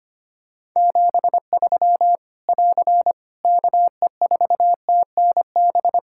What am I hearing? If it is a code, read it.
Morse code, heard as a keyed tone, about 25 wpm, 700 Hz. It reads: 73 <AR> KE4TNB
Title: K1USN SST QSO Practice 25wpm